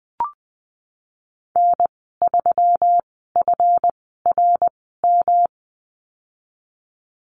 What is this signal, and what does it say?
Morse code, heard as a keyed tone, about 20 wpm, spaced as for 15 wpm, 700 Hz. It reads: N3FRM
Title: 147 Lesson 24g - Letter y - Call Signs 20wpm